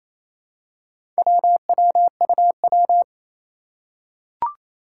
Morse code, heard as a keyed tone, about 28 wpm, 700 Hz. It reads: WWUW
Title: Four Challenging Letter-Number Combinations 28wpm